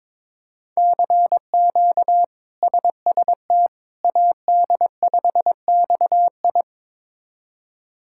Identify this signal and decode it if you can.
Morse code, heard as a keyed tone, about 22 wpm, 700 Hz. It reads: CQ SST AD5XI
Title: K1USN SST QSO Practice - 22wpm